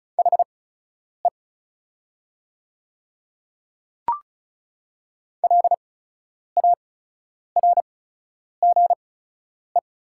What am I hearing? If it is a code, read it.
Morse code, heard as a keyed tone, about 36 wpm, spaced as for 10 wpm, 690 Hz. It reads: HE LARGE
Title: 8x Character Spacing - Top 300 Words - 35wpm